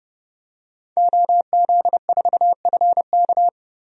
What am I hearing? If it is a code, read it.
Morse code, heard as a keyed tone, about 32 wpm, 700 Hz. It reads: OZ4FK
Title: DXCC-100 Call Signs - 5 Characters - Repeated 2x - 30wpm